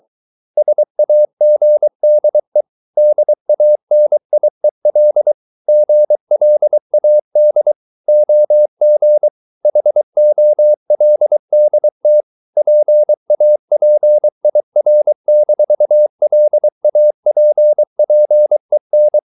Morse code, heard as a keyed tone, about 23 wpm, 600 Hz. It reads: SAGDE DANIEL GLAD OG HOLDT PAPIR-LAPPEN
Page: Ferie 23 wpm